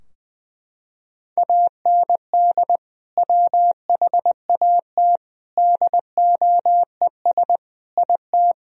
Morse code, heard as a keyed tone, about 21 wpm, 700 Hz. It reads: AND WHAT DOES IT
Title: Sentences from Top 1000 Words - Review Part 4 - 20wpm